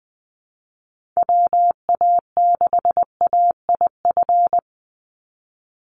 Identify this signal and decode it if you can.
Morse code, heard as a keyed tone, about 20 wpm, 700 Hz. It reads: WA6AIF
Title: USA Call Signs 20wpm